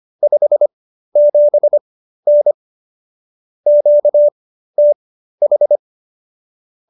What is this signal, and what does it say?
Morse code, heard as a keyed tone, about 25 wpm, spaced as for 13 wpm, 590 Hz. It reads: 57N QTH